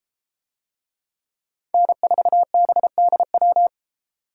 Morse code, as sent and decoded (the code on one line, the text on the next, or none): -. ....- -... -.. .--
N4BDW